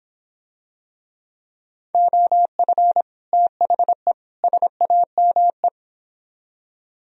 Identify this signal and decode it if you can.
Morse code, heard as a keyed tone, about 25 wpm, 700 Hz. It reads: OF THE SAME